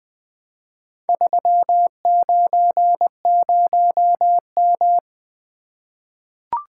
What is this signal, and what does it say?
Morse code, heard as a keyed tone, about 20 wpm, 700 Hz. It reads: 390M
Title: Four Letter-Number Combinations - 20wpm